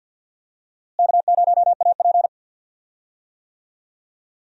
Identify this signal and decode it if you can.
Morse code, heard as a keyed tone, about 50 wpm, 700 Hz. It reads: K0AP
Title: USA Call Signs - 4 Characters - Repeated 2x - 50wpm